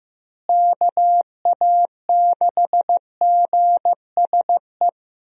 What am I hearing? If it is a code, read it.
Morse code, heard as a keyed tone, about 15 wpm, 700 Hz. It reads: KA6GSE
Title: USA Call Signs - 6 Characters - Repeated 3x - 15wpm